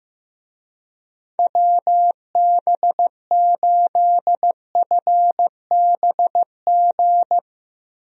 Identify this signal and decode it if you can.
Morse code, heard as a keyed tone, about 15 wpm, 700 Hz. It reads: WB8FBG
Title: USA Call Signs - 6 Characters - Repeated 2x - 15wpm